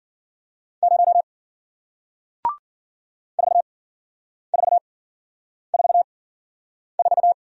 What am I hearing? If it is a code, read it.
Morse code, heard as a keyed tone, about 53 wpm, spaced as for 9 wpm, 700 Hz. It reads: 0 4444